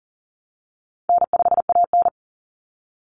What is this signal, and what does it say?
Morse code, heard as a keyed tone, about 40 wpm, 700 Hz. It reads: N5AN
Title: Easy Call Signs - 40wpm